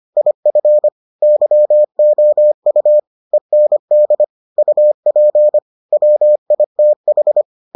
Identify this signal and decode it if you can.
Morse code, heard as a keyed tone, about 25 wpm, 600 Hz. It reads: IF YOU END UP WITH